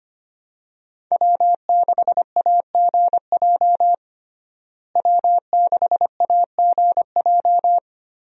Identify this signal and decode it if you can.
Morse code, heard as a keyed tone, about 25 wpm, 700 Hz. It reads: W6AGJ W6AGJ
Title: USA Call Signs - 5 Characters - Repeated 3x - 25wpm